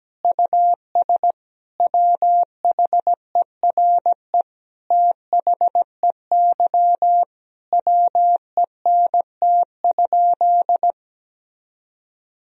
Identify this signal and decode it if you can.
Morse code, heard as a keyed tone, about 17 wpm, 700 Hz. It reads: US WHERE THEY WENT?
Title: Sentences from Top 1000 Words - Review Part 3 - 17wpm